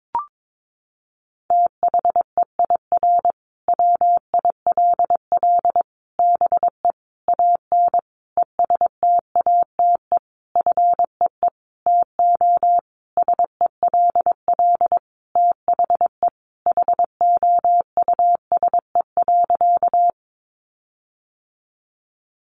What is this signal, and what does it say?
Morse code, heard as a keyed tone, about 22 wpm, 700 Hz. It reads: THEIR WILL BE AN ESTATE FEE TO SELL THE HOUSE.
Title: Sentences from Top 1500 Words 22wpm